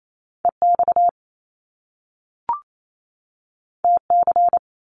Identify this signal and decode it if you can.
Morse code, heard as a keyed tone, about 28 wpm, 700 Hz. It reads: EX TC
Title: Top Two Letter Combinations in Words 28wpm